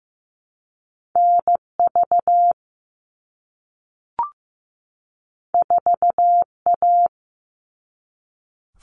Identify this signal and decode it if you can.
Morse code, heard as a keyed tone, about 15 wpm, 700 Hz. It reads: NV 4A
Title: All Cut Numbers - 2-Digit 15wpm